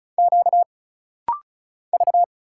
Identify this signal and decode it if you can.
Morse code, heard as a keyed tone, about 36 wpm, 700 Hz. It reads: Q V
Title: Single Letter-Number Alt - Rapid-Fire 35wpm